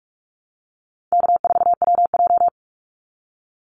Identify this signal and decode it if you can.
Morse code, heard as a keyed tone, about 44 wpm, 700 Hz. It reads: K4WJ
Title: Easy Call Signs Repeated 2x - 45wpm